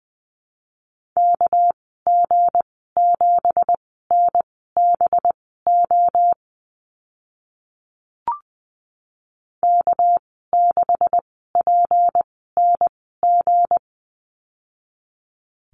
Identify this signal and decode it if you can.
Morse code, heard as a keyed tone, about 20 wpm, spaced as for 14 wpm, 700 Hz. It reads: KG7NBO K6PNG
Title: 231 Lesson 37g - Letter x - Call Signs 20wpm